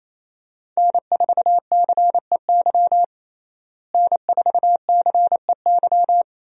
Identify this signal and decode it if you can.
Morse code, heard as a keyed tone, about 28 wpm, 700 Hz. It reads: N4CEY N4CEY
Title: USA Call Signs - 5 Characters - Repeated 2x - 28wpm